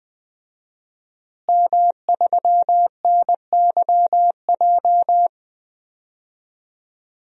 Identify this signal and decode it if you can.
Morse code, heard as a keyed tone, about 20 wpm, 700 Hz. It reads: M3NYJ
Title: DXCC-100 Call Signs - 5 Characters - Repeated 2x - 20wpm